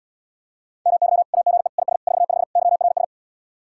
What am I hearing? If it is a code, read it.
Morse code, heard as a keyed tone, about 79 wpm, 700 Hz. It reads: MY AGE IS 45 YRS